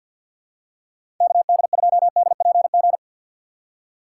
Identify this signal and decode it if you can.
Morse code, heard as a keyed tone, about 52 wpm, 700 Hz. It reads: KD2DPG
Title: USA Call Signs - 6 Characters - 6x Character Spacing - 50wpm